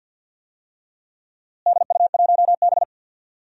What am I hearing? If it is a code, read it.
Morse code, heard as a keyed tone, about 50 wpm, 700 Hz. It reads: DU1B
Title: DXCC-100 Call Signs - 4 Characters - Repeated 3x - 50wpm